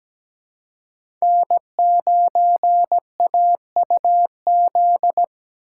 Morse code, heard as a keyed tone, about 17 wpm, 700 Hz. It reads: N9AUZ
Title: USA Call Signs - 5 Characters - 2x Character Spacing - 17wpm